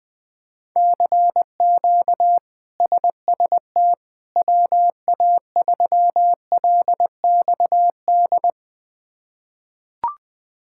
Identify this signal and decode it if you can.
Morse code, heard as a keyed tone, about 21 wpm, 700 Hz. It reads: CQ SST WA3LXD
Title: K1USN SST QSO Practice - 20wpm